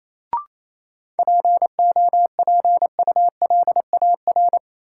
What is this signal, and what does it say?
Morse code, heard as a keyed tone, about 28 wpm, 700 Hz. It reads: POPULAR
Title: Sentences from Top 1000 Words - Review Part 3 - New Word Drill - 28wpm